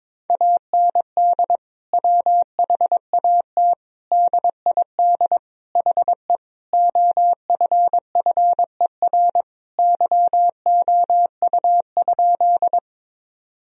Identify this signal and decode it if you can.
Morse code, heard as a keyed tone, about 22 wpm, 700 Hz. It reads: AND WHAT DID HE OFFER YOU?
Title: Sentences from Top 1000 Words - Review Part 4 - 22wpm